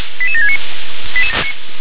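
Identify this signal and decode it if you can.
Five-tone selective call (ZVEI), heard as quick runs of five tones, tones of about 70 ms. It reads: 88769
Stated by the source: ZVEI five-tone call received by SDR.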